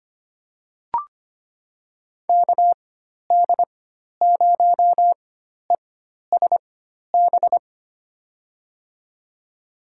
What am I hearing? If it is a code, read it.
Morse code, heard as a keyed tone, about 25 wpm, spaced as for 12 wpm, 700 Hz. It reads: KD0ESB
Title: USA Call Signs - 6 Characters - 4x Character Spacing - 25wpm